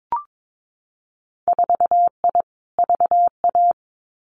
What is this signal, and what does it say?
Morse code, heard as a keyed tone, about 22 wpm, 700 Hz. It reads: 4I VA
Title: Winter Field Day Exchanges 22wpm